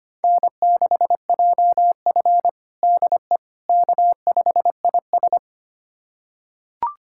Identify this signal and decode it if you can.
Morse code, heard as a keyed tone, about 25 wpm, 700 Hz. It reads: N6JF DE K5IS
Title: USA Callsign DE Callsign - 4 Characters - 25wpm